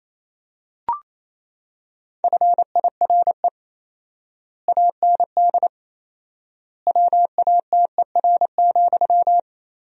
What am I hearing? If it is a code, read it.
Morse code, heard as a keyed tone, about 28 wpm, 700 Hz. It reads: FIRE AND WATER,
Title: 4x Word-Spacing - Sentences from Top 400 Words - 28wpm